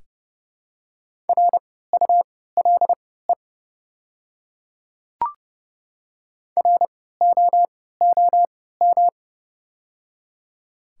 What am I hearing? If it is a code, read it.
Morse code, heard as a keyed tone, about 32 wpm, spaced as for 17 wpm, 700 Hz. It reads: RULE ROOM
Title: 3x Character Spacing - Top 500 Words - 30wpm